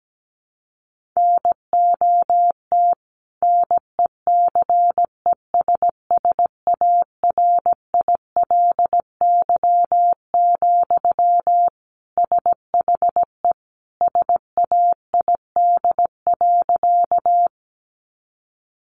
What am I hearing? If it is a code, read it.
Morse code, heard as a keyed tone, about 17 wpm, 700 Hz. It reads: NOT NECESSARILY, SHE SAID.
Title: Sentences from Top 1400 Words 17wpm